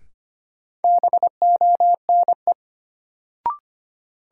Morse code, heard as a keyed tone, about 25 wpm, 700 Hz. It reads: BONE
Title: Sentences from Top 1600 Words - New Word Drill Alt - 25wpm